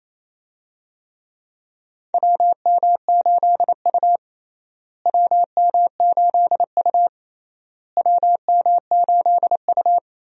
Morse code, heard as a keyed tone, about 28 wpm, 700 Hz. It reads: WM8U WM8U WM8U
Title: USA Call Signs - 4 Characters - Repeated 3x - 28wpm